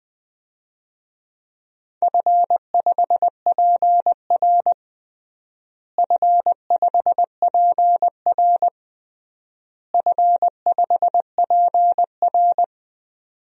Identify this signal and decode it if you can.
Morse code, heard as a keyed tone, about 21 wpm, 700 Hz. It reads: F5PR F5PR F5PR
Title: DXCC-100 Call Signs - 4 Characters - Repeated 3x - 20wpm